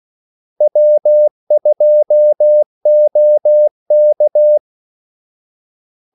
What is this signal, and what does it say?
Morse code, heard as a keyed tone, about 16 wpm, 600 Hz. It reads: W2OK
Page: Kaldesignaler 16 wpm